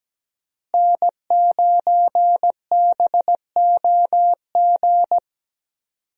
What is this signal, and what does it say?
Morse code, heard as a keyed tone, about 17 wpm, 700 Hz. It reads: N9BOG
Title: USA Call Signs - 5 Characters - Repeated 3x - 17wpm